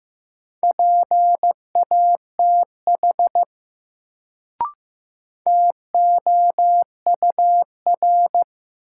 Morse code, heard as a keyed tone, about 15 wpm, 700 Hz. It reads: PATH TOUR
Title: Sentences from Top 1400 Words - New Word Drill Alt - 15wpm